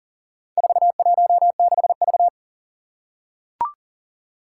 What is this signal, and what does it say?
Morse code, heard as a keyed tone, about 40 wpm, 700 Hz. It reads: 416V
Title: Four Challenging Letter-Number Combinations 40wpm